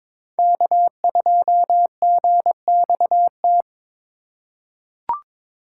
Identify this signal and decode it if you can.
Morse code, heard as a keyed tone, about 22 wpm, 700 Hz. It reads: K2GXT
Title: USA Call Signs - 5 Characters - 4x Character Spacing - 22wpm